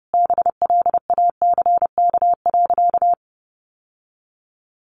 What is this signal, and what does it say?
Morse code, heard as a keyed tone, about 30 wpm, 700 Hz. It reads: BLACK.
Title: Sentences from Top 1200 Words 30wpm